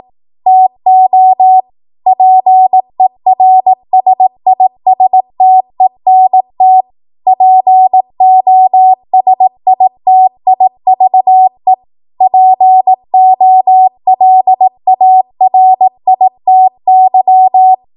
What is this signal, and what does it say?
Morse code, heard as a keyed tone, about 18 wpm, 750 Hz. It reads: TO PERSISTENT POSITIVE POLARITY
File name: Morse / 18WPM